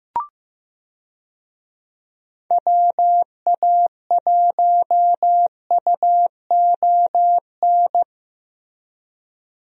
Morse code, heard as a keyed tone, about 15 wpm, 700 Hz. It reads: WA1UON
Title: USA Call Signs - 6 Characters - Repeated 2x - 15wpm